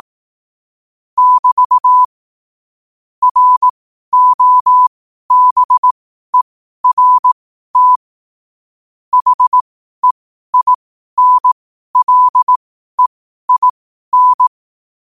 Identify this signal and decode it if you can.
Morse code, heard as a keyed tone, about 18 wpm, spaced as for 12 wpm, 1000 Hz. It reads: <BT> ROBERT HEINLEIN
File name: qotdfiles / 12wpm